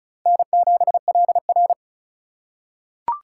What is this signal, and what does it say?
Morse code, heard as a keyed tone, about 34 wpm, 700 Hz. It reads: N7LR
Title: K1USN SST QSO Practice - 35wpm